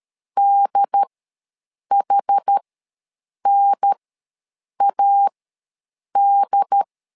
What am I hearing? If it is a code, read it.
Morse code, heard as a keyed tone, about 13 wpm, spaced as for 7 wpm, 800 Hz. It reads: DHNAD
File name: LEZIONI / 2 UHDN / 2 UHDN ws 12wpm 1708